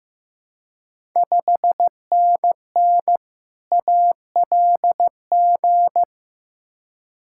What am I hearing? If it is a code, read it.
Morse code, heard as a keyed tone, about 15 wpm, 700 Hz. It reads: 5NN ALG
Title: Ontario QSO Party Exchange 15wpm